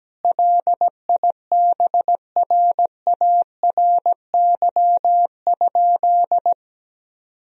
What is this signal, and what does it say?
Morse code, heard as a keyed tone, about 17 wpm, 700 Hz. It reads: LIBRARY?
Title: Sentences from Top 1000 Words - Review Part 4 - 17wpm